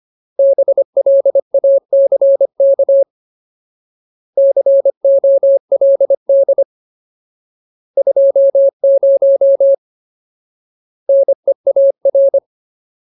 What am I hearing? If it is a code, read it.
Morse code, heard as a keyed tone, about 25 wpm, 550 Hz. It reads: BLACK COLD 20 NEAR
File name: Ultimate Word List / 25WPM Ultimate